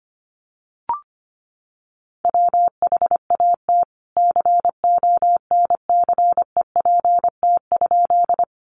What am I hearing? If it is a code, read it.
Morse code, heard as a keyed tone, about 25 wpm, 700 Hz. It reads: WHAT CONCEPT?
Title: Sentences from Top 1100 Words 25wpm